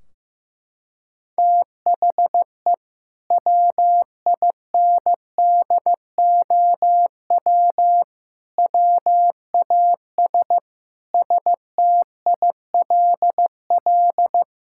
Morse code, heard as a keyed tone, about 15 wpm, 700 Hz. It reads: THE WINDOW WAS STILL
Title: Sentences from Top 1600 Words 15wpm